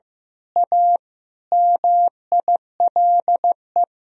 Morse code, heard as a keyed tone, about 15 wpm, 700 Hz. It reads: A MILE
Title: Sentences from Top 1000 Words - Review Part 3 - 15wpm